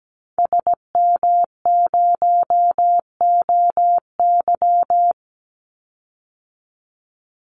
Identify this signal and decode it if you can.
Morse code, heard as a keyed tone, about 17 wpm, 700 Hz. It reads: SM0OY
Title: Call Signs - 17wpm